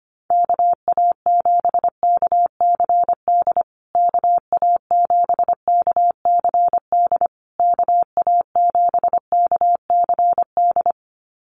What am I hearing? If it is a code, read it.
Morse code, heard as a keyed tone, about 25 wpm, 700 Hz. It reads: KA7KCD KA7KCD KA7KCD
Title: USA Call Signs Repeated 3x 25wpm